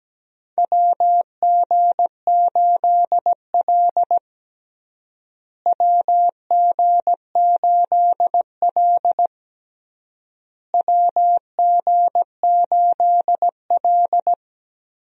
Morse code, heard as a keyed tone, about 17 wpm, 700 Hz. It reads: WG8L WG8L WG8L
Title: USA Call Signs - 4 Characters - Repeated 3x - 17wpm